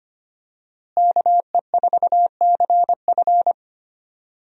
Morse code, heard as a keyed tone, about 25 wpm, 700 Hz. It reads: KE4CF
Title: USA Call Signs - 5 Characters - Repeated 2x - 25wpm